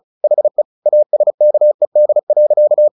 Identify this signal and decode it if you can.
Morse code, heard as a keyed tone, about 34 wpm, 600 Hz. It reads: HE ASKED.
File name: the old man and the sea / the old man and the sea 1 35wpm